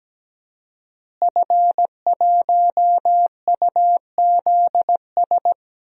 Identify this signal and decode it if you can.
Morse code, heard as a keyed tone, about 17 wpm, 700 Hz. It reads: F1UZS
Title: DXCC-100 Call Signs - 5 Characters - Repeated 3x - 17wpm